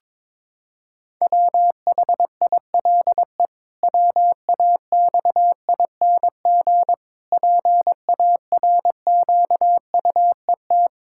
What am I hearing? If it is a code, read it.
Morse code, heard as a keyed tone, about 22 wpm, 700 Hz. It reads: WHILE WAXING PARQUET
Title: Pangram Challenge v2 - 22wpm